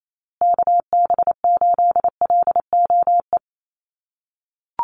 Morse code, heard as a keyed tone, about 28 wpm, 700 Hz. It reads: KB8LOE
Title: USA Call Signs Repeated 2x 28wpm